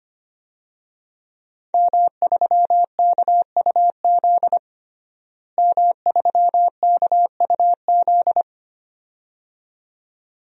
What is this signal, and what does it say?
Morse code, heard as a keyed tone, about 25 wpm, 700 Hz. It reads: M3KUZ M3KUZ
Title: DXCC-100 Call Signs - 5 Characters - Repeated 2x - 25wpm